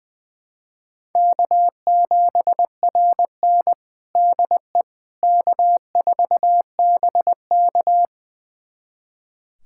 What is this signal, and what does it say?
Morse code, heard as a keyed tone, about 21 wpm, 700 Hz. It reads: K7RN DE K4BK
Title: USA Callsign DE Callsign - 4 Characters - 20wpm